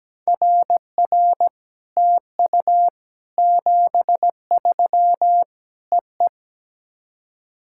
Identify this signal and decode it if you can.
Morse code, heard as a keyed tone, about 17 wpm, 700 Hz. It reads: RR TU 73 EE
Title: SOTA QSO Practice 17wpm